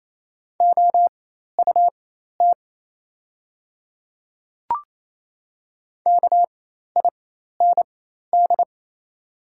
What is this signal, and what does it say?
Morse code, heard as a keyed tone, about 28 wpm, spaced as for 13 wpm, 690 Hz. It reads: OUT KIND
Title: 4x Character Spacing - Top 300 Words - 28wpm